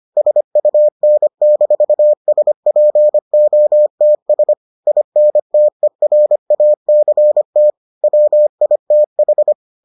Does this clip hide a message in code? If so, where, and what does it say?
Yes: Morse code from 0.2 to 9.5 s, heard as a keyed tone, about 25 wpm, 600 Hz. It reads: SUN-SPOTS INTERACT WITH